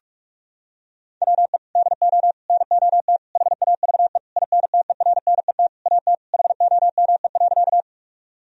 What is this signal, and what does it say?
Morse code, heard as a keyed tone, about 41 wpm, 700 Hz. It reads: WE DO NOT HAVE INTERNET AT HOME.
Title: Sentences from Top 1000 Words - Review Part 4 - 45wpm